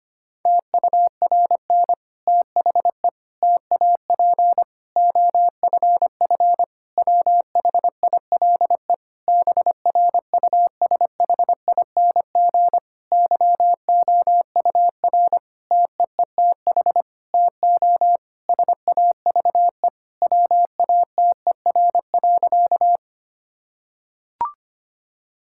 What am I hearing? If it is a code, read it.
Morse code, heard as a keyed tone, about 25 wpm, 700 Hz. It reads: TURN THE TAP OFF WHILE BRUSHING YOUR TEETH TO SAVE WATER.
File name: practice / 25wpm / Difficult Sentences with Hints - 25wpm